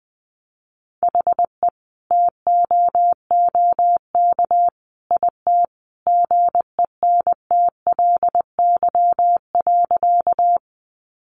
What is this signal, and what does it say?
Morse code, heard as a keyed tone, about 20 wpm, 700 Hz. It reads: HE TOOK IT GENTLY.
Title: Sentences from Top 1500 Words 20wpm